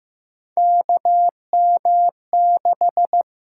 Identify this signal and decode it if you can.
Morse code, heard as a keyed tone, about 15 wpm, 700 Hz. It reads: KM6
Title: USA Call Signs - 6 Characters - 8x Character Spacing - 15wpm